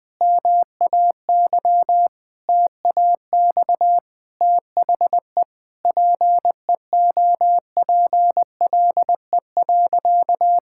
Morse code, heard as a keyed tone, about 20 wpm, 700 Hz. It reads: MAY TAX THE PEOPLE.
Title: Sentences from Top 1000 Words - Review Part 5 - 20wpm